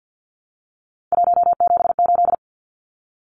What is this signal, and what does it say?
Morse code, heard as a keyed tone, about 50 wpm, 700 Hz. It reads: 178
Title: Numbers - 3-Digit 50wpm